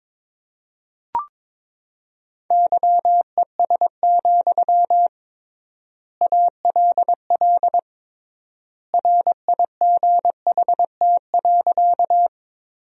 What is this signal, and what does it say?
Morse code, heard as a keyed tone, about 22 wpm, 700 Hz. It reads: YES, ALL RIGHT.
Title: 3x Word-Spacing - Sentences from Top 500 Words - 22wpm